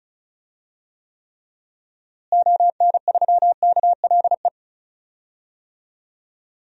Morse code, heard as a keyed tone, about 36 wpm, 700 Hz. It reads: ON3KLE
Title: DXCC-100 Call Signs - 6 Characters - Repeated 2x - 35wpm